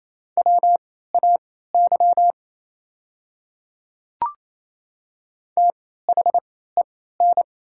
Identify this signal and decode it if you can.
Morse code, heard as a keyed tone, about 28 wpm, spaced as for 16 wpm, 700 Hz. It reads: WAY THEN
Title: 3x Character Spacing - Top 500 Words - 28wpm